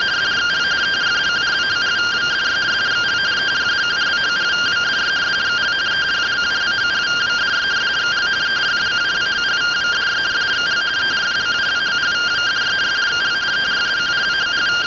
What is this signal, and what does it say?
Two-tone teleprinter signal, about 170 Hz shift.